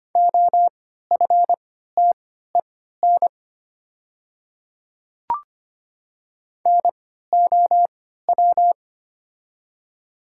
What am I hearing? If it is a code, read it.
Morse code, heard as a keyed tone, about 25 wpm, spaced as for 14 wpm, 700 Hz. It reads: OFTEN NOW
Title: 3x Character Spacing - Top 300 Words - 25wpm